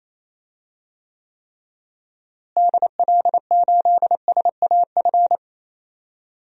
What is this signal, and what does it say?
Morse code, heard as a keyed tone, about 27 wpm, 700 Hz. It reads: DL8SAF
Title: DXCC-100 Call Signs - 6 Characters - Repeated 3x - 28wpm